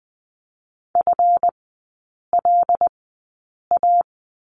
Morse code, heard as a keyed tone, about 20 wpm, spaced as for 8 wpm, 700 Hz. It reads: FLA
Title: 218 Lesson 35e - Number 0 - Three Characters 20wpm